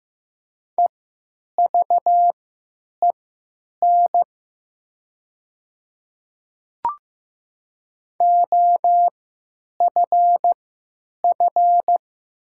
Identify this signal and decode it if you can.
Morse code, heard as a keyed tone, about 15 wpm, spaced as for 8 wpm, 700 Hz. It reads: EVEN OFF
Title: 3x Character Spacing - Top 200 Words - 15wpm